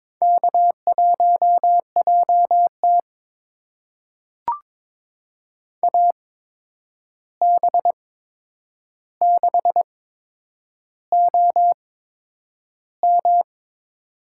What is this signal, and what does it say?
Morse code, heard as a keyed tone, about 22 wpm, spaced as for 9 wpm, 700 Hz. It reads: K1JT AB6OM
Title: USA Call Signs - 8x Character Spacing - 22wpm